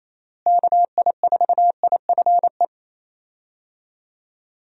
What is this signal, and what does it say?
Morse code, heard as a keyed tone, about 28 wpm, 700 Hz. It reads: KI4IFE
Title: USA Call Signs - 6 Characters - Repeated 3x - 28wpm